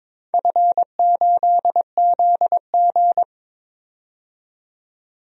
Morse code, heard as a keyed tone, about 22 wpm, 700 Hz. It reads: F8ZG
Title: DXCC-100 Call Signs - 4 Characters - 22wpm